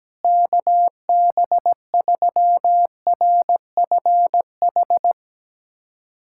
Morse code, heard as a keyed tone, about 17 wpm, 700 Hz. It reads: KB3RFH
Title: USA Callsign DE Callsign 17wpm